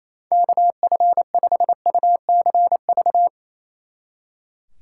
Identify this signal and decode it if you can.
Morse code, heard as a keyed tone, about 34 wpm, 700 Hz. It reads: KF5UCV
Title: Speed Racing - USA 2x3 Call Signs 28wpm